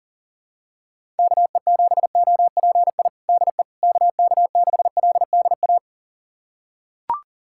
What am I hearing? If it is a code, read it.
Morse code, heard as a keyed tone, about 37 wpm, 700 Hz. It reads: KE7OPI DE KK6LDA
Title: USA Callsign DE Callsign - 6 Characters - 40wpm